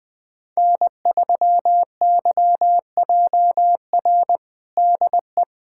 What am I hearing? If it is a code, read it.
Morse code, heard as a keyed tone, about 20 wpm, 700 Hz. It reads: N3YJR DE
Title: USA Callsign DE Callsign - 5 Characters - 20wpm